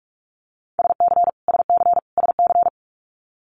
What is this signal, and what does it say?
Morse code, heard as a keyed tone, about 46 wpm, 700 Hz. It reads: SC SC SC
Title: Multi-letter Phonemes Repeated 3x 45wpm